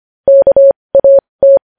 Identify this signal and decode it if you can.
Morse code, heard as a keyed tone, about 25 wpm, spaced as for 20 wpm, 550 Hz. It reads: KAT
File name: podcast / News headlines-1x-25x20